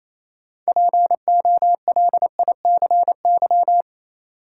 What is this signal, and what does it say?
Morse code, heard as a keyed tone, about 28 wpm, 700 Hz. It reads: POLICY
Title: Sentences from Top 1000 Words - Review Part 5 - New Word Drill - 28wpm